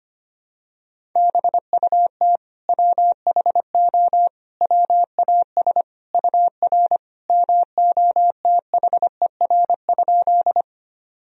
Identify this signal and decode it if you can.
Morse code, heard as a keyed tone, about 25 wpm, 700 Hz. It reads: BUT WHO WAS UR MOTHER?
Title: Sentences from Top 200 Words and Abbreviations 25wpm